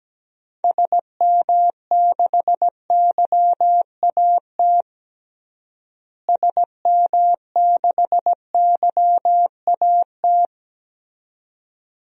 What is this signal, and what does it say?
Morse code, heard as a keyed tone, about 17 wpm, 700 Hz. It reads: SM6YAT SM6YAT
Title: DXCC-100 Call Signs - 6 Characters - Repeated 2x - 17wpm